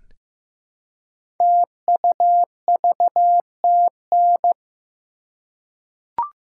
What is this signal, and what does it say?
Morse code, heard as a keyed tone, about 15 wpm, 700 Hz. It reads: TUVTN
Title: Code Groups - 5 Letters - 15wpm